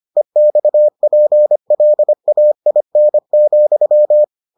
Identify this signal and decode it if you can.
Morse code, heard as a keyed tone, about 25 wpm, 600 Hz. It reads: EXPLAIN,